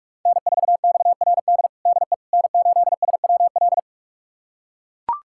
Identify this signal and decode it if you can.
Morse code, heard as a keyed tone, about 41 wpm, 700 Hz. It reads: N4XRD DE N8SWL
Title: USA Callsign DE Callsign 45wpm